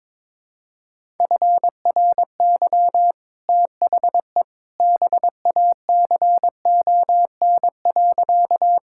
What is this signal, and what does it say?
Morse code, heard as a keyed tone, about 22 wpm, 700 Hz. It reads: FRY THE BACON.
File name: practice / 22wpm / Easy Sentences with Hints - 22wpm